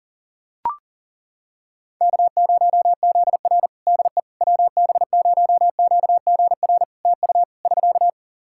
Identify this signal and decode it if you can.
Morse code, heard as a keyed tone, about 38 wpm, 700 Hz. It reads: K0ZR DE WB0QGR TU <SK>
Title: Practice QSO - Exchange 4 - Ending - 40wpm